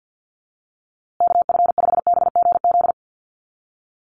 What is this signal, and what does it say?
Morse code, heard as a keyed tone, about 50 wpm, 700 Hz. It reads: KF5BGZ
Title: USA Call Signs Repeated 3x 50wpm